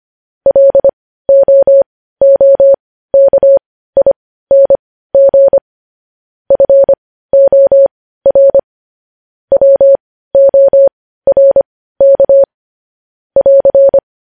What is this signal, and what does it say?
Morse code, heard as a keyed tone, about 25 wpm, spaced as for 15 wpm, 550 Hz. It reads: LOOKING FOR WORK <AR>
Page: news headlines in morse code at 15 wpm